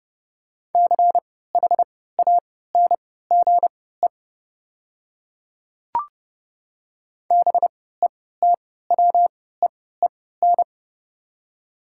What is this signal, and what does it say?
Morse code, heard as a keyed tone, about 30 wpm, spaced as for 17 wpm, 700 Hz. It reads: CHANGE BETWEEN
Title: 3x Character Spacing - Top 300 Words - 30wpm